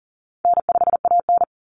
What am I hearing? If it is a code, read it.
Morse code, heard as a keyed tone, about 40 wpm, 700 Hz. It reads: N5AN
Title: Call Signs Repeated 2x - 40wpm